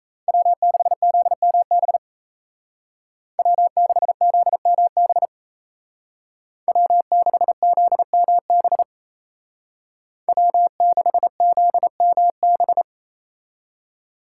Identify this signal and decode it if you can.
Morse code, heard as a keyed tone, about 36 wpm, 700 Hz. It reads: W6ZMB W6ZMB W6ZMB W6ZMB